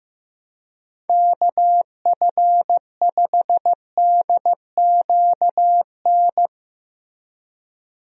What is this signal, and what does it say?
Morse code, heard as a keyed tone, about 15 wpm, 700 Hz. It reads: KF5DQN
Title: USA Call Signs - 6 Characters - Repeated 2x - 15wpm